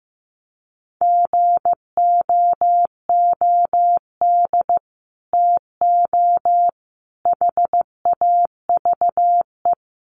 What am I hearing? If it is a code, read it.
Morse code, heard as a keyed tone, about 15 wpm, 700 Hz. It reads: GOOD TO HAVE
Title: Sets of 5 Words Encore2 15wpm